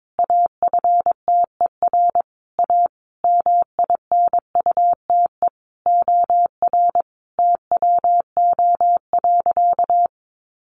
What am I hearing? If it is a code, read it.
Morse code, heard as a keyed tone, about 22 wpm, 700 Hz. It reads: AFTER A MINUTE OR TWO.
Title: Sentences from Top 1400 Words 22wpm